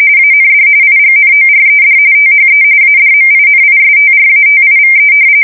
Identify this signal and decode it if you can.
RTTY, about 45.45 baud. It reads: AND IGNORED THE ADMINISTRATIVE
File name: fsk170 / usnavy4